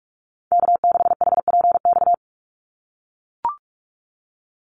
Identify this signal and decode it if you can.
Morse code, heard as a keyed tone, about 44 wpm, 700 Hz. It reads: K6HPX
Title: NA SOTA Activator Callsigns 45wpm